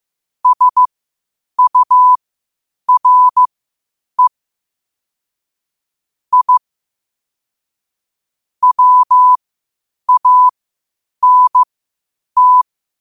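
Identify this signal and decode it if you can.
Morse code, heard as a keyed tone, about 15 wpm, spaced as for 8 wpm, 1000 Hz. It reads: SURE I WANT
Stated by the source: Morse code quotes at 8 words per minute from QOTD at 8 WPM.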